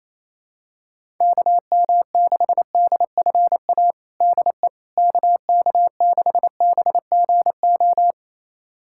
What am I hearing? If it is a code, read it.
Morse code, heard as a keyed tone, about 28 wpm, 700 Hz. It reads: KM6DFA DE KK6BGO
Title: USA Callsign DE Callsign - 6 Characters - 28wpm